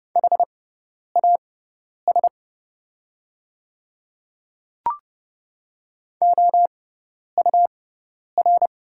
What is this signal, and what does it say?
Morse code, heard as a keyed tone, about 30 wpm, spaced as for 10 wpm, 700 Hz. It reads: HAS OUR
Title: 6x Character Spacing - Top 200 Words - 30wpm